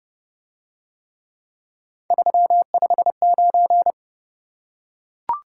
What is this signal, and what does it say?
Morse code, heard as a keyed tone, about 28 wpm, 700 Hz. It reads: 359